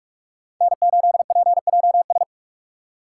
Morse code, heard as a keyed tone, about 44 wpm, 700 Hz. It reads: N8PJS